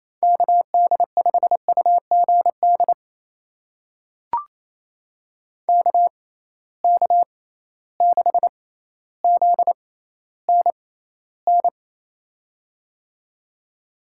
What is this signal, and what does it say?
Morse code, heard as a keyed tone, about 27 wpm, spaced as for 14 wpm, 700 Hz. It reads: KD5UGD KK6ZNN
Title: USA Call Signs - 6 Characters - 6x Character Spacing - 28wpm